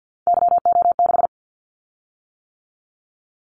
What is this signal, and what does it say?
Morse code, heard as a keyed tone, about 49 wpm, 700 Hz. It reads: YO6
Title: Callsign Prefixes for Top 25 DXCC - 50wpm